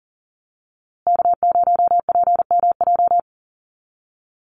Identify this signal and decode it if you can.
Morse code, heard as a keyed tone, about 40 wpm, 700 Hz. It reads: K0PMJ
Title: USA Call Signs Repeated 3x 40wpm